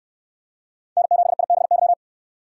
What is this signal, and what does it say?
Morse code, heard as a keyed tone, about 69 wpm, 700 Hz. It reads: N7IBY